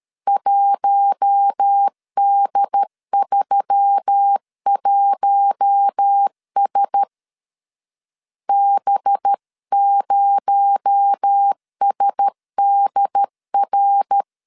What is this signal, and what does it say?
Morse code, heard as a keyed tone, about 13 wpm, 800 Hz. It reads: 1D31S B0SDR
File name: CW / DET / 1 6 1500C S A 12wpm 3440